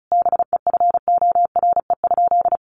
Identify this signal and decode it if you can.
Morse code, heard as a keyed tone, about 34 wpm, 700 Hz. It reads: BEFORE?
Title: Sentences from Top 900 Words 35wpm